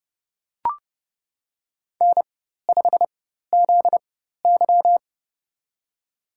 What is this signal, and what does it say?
Morse code, heard as a keyed tone, about 30 wpm, spaced as for 14 wpm, 700 Hz. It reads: N5ZY
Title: USA Call Signs - 4 Characters - 4x Character Spacing - 30wpm